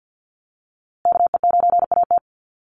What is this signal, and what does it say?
Morse code, heard as a keyed tone, about 50 wpm, 700 Hz. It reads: KE9AT
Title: USA Call Signs Repeated 2x 50wpm